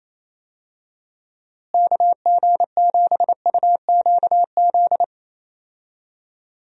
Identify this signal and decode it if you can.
Morse code, heard as a keyed tone, about 27 wpm, 700 Hz. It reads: KG7UQZ
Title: USA Call Signs - 6 Characters - Repeated 2x - 28wpm